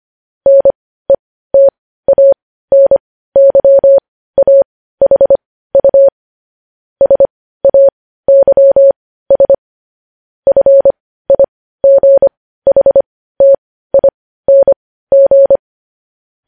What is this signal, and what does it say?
Morse code, heard as a keyed tone, about 25 wpm, spaced as for 15 wpm, 550 Hz. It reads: NETANYAHU SAYS FIGHTING
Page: news headlines in morse code at 15 wpm